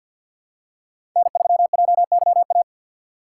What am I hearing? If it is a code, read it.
Morse code, heard as a keyed tone, about 48 wpm, 700 Hz. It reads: N3JYA
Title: USA Call Signs - 2x Character Spacing - 50wpm